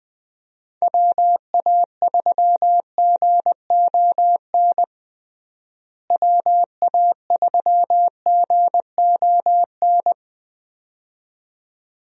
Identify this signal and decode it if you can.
Morse code, heard as a keyed tone, about 21 wpm, 700 Hz. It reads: WA3GON WA3GON
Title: USA Call Signs - 6 Characters - Repeated 2x - 20wpm